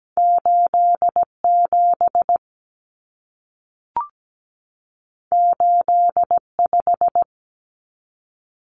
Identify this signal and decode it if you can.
Morse code, heard as a keyed tone, about 17 wpm, 700 Hz. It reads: 87 85
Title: Common Cut Numbers - 2-Digit 17wpm